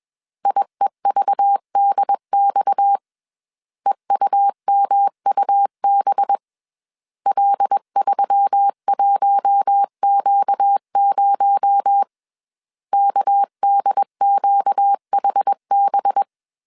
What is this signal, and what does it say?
Morse code, heard as a keyed tone, about 21 wpm, 800 Hz. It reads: IEVDX EUMUB L31Q0 KDQHB
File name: CW / DET / 1 6 1500C S B 20wpm 2048